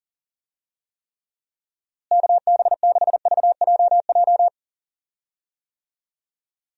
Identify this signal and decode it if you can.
Morse code, heard as a keyed tone, about 38 wpm, 700 Hz. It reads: KB6VJJ
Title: USA Call Signs - 6 Characters - Repeated 3x - 40wpm